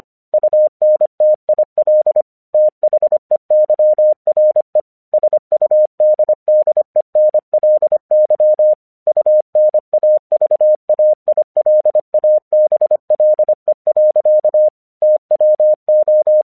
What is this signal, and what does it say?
Morse code, heard as a keyed tone, about 25 wpm, 610 Hz. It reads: UNTIL THEYRE SUDDENLY UNAVAILABLE. TWO
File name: 1 English 25wpm / AQC06